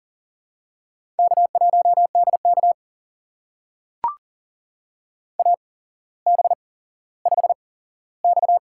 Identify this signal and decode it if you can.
Morse code, heard as a keyed tone, about 38 wpm, spaced as for 17 wpm, 700 Hz. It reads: K1DK AB5X
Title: USA Call Signs - 4 Characters - 8x Character Spacing - 40wpm